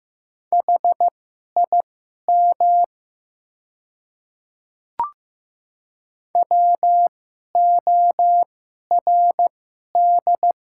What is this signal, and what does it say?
Morse code, heard as a keyed tone, about 15 wpm, spaced as for 11 wpm, 700 Hz. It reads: HIM WORD
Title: 2x Character Spacing - Top 100 Words - 15wpm